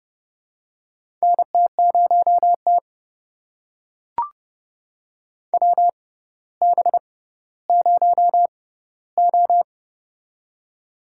Morse code, heard as a keyed tone, about 30 wpm, spaced as for 15 wpm, 700 Hz. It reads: NT0T WB0O
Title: USA Call Signs - 4 Characters - 6x Character Spacing - 30wpm